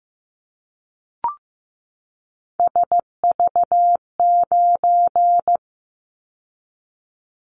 Morse code, heard as a keyed tone, about 15 wpm, 700 Hz. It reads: SV9
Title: Callsign Prefixes for Top 100 DXCC - 15wpm